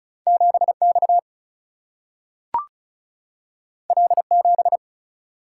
Morse code, heard as a keyed tone, about 36 wpm, 700 Hz. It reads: 7X L7
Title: Single-Letter Single-Number 35wpm